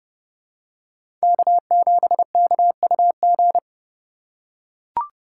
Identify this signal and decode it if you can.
Morse code, heard as a keyed tone, about 29 wpm, 700 Hz. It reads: K7KUG